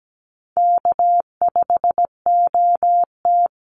Morse code, heard as a keyed tone, about 17 wpm, 700 Hz. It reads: K5OT
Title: Easy Call Signs - 17wpm